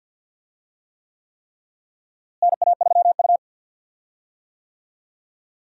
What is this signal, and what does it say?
Morse code, heard as a keyed tone, about 46 wpm, 700 Hz. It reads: NA3U
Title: USA Call Signs - 4 Characters - Repeated 3x - 50wpm